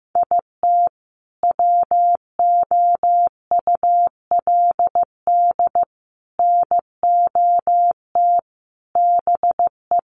Morse code, heard as a keyed tone, about 15 wpm, 700 Hz. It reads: IT WOULD NOT BE
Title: Sentences from Top 600 Words 15wpm